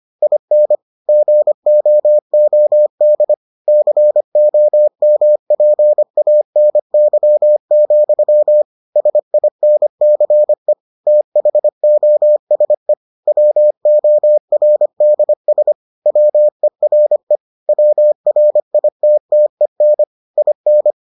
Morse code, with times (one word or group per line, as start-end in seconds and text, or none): 0.2-0.8 IN
1.1-3.3 GOOD
3.7-8.6 COMPANY,
9.0-10.7 SINCE
11.1-12.9 THOSE
13.3-15.7 WORDS
16.1-17.4 WERE
17.7-20.0 WRITTEN
20.4-20.9 IN